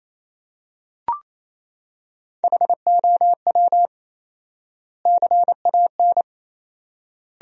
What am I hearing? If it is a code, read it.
Morse code, heard as a keyed tone, about 28 wpm, 700 Hz. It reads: HOW CAN
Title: 4x Word-Spacing - Sentences from Top 200 Words - 28wpm